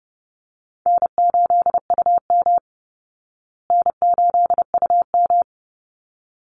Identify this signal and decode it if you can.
Morse code, heard as a keyed tone, about 30 wpm, 700 Hz. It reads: N8UM N8UM
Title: Easy Call Signs Repeated 2x - 30wpm